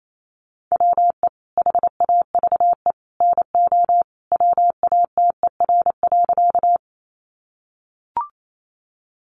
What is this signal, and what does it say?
Morse code, heard as a keyed tone, about 28 wpm, 700 Hz. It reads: WE HAVE NO WATER.
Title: Sentences from Top 100 Words 28wpm